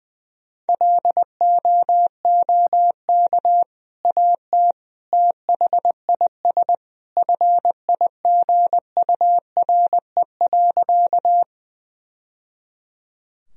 Morse code, heard as a keyed tone, about 20 wpm, 700 Hz. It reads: LOOK AT THIS FIGURE.
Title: Sentences from Top 1000 Words - Review Part 2 - 20wpm